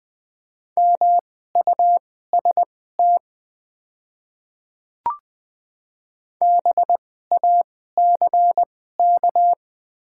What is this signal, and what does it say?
Morse code, heard as a keyed tone, about 20 wpm, spaced as for 14 wpm, 700 Hz. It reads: MUST BACK
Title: 2x Character Spacing - Top 200 Words - 20wpm